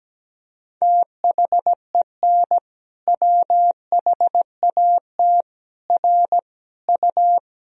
Morse code, heard as a keyed tone, about 17 wpm, 700 Hz. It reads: THEN WHAT R U
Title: Sentences from Top 100 Words and Abbreviations 17wpm